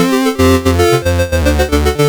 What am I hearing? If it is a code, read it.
There is no code.